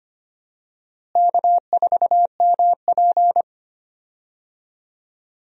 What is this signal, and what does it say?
Morse code, heard as a keyed tone, about 25 wpm, 700 Hz. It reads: K4MP
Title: USA Call Signs - 4 Characters - Repeated 3x - 25wpm